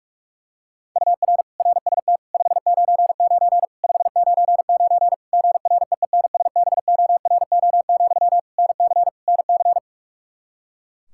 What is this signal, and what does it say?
Morse code, heard as a keyed tone, about 41 wpm, 700 Hz. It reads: UR RST 599 599 GREENSBORO, NC NC
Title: K1USN SST QSO Practice 45wpm